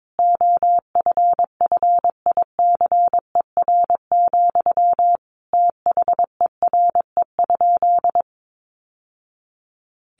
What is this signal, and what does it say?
Morse code, heard as a keyed tone, about 22 wpm, 700 Hz. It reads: OFFICER, THERE?
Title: Sentences from Top 1200 Words 22wpm